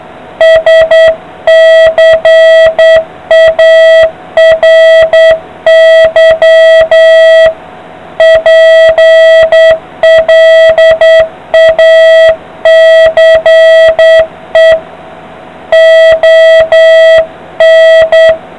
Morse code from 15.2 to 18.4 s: ON